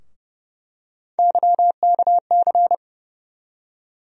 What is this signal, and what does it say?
Morse code, heard as a keyed tone, about 30 wpm, 700 Hz. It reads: YKC
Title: USA Call Sign Postfixes - 3 Letters 30wpm